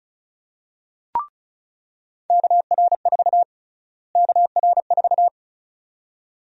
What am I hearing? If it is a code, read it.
Morse code, heard as a keyed tone, about 36 wpm, 700 Hz. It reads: KR4 KR4
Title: USA Call Sign Prefixes - Repeated 2x - 35wpm